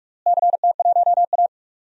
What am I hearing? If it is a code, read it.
Morse code, heard as a keyed tone, about 46 wpm, 690 Hz. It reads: CT1A
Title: DXCC-100 Call Signs - 4 Characters - Repeated 3x - 45wpm